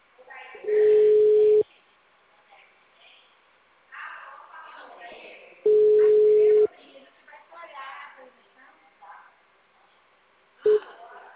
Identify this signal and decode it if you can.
Ringback tone (Europe, 425 Hz), with speech.
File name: out- -2044- - - .46221